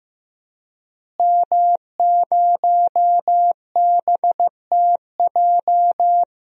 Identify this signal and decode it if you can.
Morse code, heard as a keyed tone, about 16 wpm, 700 Hz. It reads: M0BTJ
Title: DXCC-100 Call Signs - 5 Characters - Repeated 3x - 15wpm